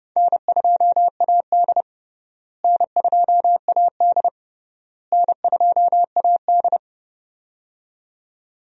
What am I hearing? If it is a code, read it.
Morse code, heard as a keyed tone, about 32 wpm, 700 Hz. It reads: N2AD N2AD N2AD
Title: USA Call Signs - 4 Characters - Repeated 3x - 30wpm